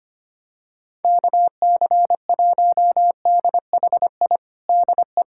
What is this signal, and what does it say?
Morse code, heard as a keyed tone, about 25 wpm, 700 Hz. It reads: KC1DHI DE
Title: USA Callsign DE Callsign - 6 Characters - 25wpm